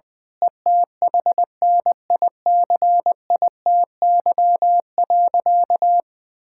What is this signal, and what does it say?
Morse code, heard as a keyed tone, about 21 wpm, 700 Hz. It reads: ETHNICITY.
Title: Sentences with 9-Word Phrases - 20wpm